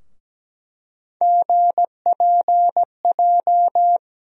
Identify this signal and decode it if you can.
Morse code, heard as a keyed tone, about 17 wpm, 700 Hz. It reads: GPJ
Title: USA Call Sign Postfixes - 3 Letters 17wpm